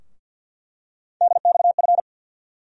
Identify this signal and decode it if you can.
Morse code, heard as a keyed tone, about 47 wpm, 700 Hz. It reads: DXF
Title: USA Call Sign Postfixes - 3 Letters 50wpm